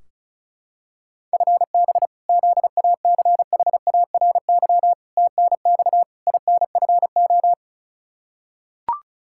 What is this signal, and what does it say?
Morse code, heard as a keyed tone, about 36 wpm, 700 Hz. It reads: FB ZACHARY TNX INFO
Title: Practice QSO - Exchange 3 - More Info - 35wpm